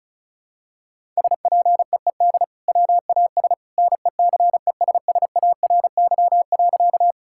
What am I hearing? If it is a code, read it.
Morse code, heard as a keyed tone, about 36 wpm, 700 Hz. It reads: SPEED WAS NECESSARY.
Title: Sentences from Top 1000 Words - Review Part 5 - 35wpm